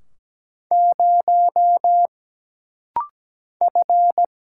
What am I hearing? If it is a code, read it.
Morse code, heard as a keyed tone, about 17 wpm, 700 Hz. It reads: 0 F
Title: Single Letter-Number - Rapid-Fire 17wpm